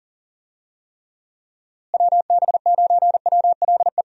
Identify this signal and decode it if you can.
Morse code, heard as a keyed tone, about 37 wpm, 700 Hz. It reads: WB9WLE
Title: USA Call Signs - 6 Characters - Repeated 3x - 40wpm